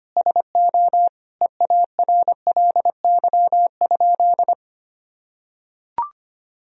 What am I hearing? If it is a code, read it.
Morse code, heard as a keyed tone, about 25 wpm, 700 Hz. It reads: SO EARLY?
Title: Sentences from Top 400 Words and Abbreviations 25wpm